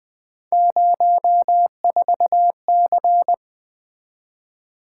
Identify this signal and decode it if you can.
Morse code, heard as a keyed tone, about 20 wpm, 700 Hz. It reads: 04C
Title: Three Letter-Number Combinations - 20wpm